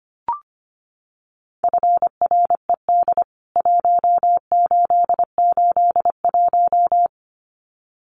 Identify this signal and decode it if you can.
Morse code, heard as a keyed tone, about 25 wpm, 700 Hz. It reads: FRED 1881
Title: CWT Exchange 25wpm